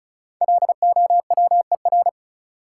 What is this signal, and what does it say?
Morse code, heard as a keyed tone, about 36 wpm, 700 Hz. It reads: LOWER
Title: Sentences from Top 700 Words - New Word Drill - 35wpm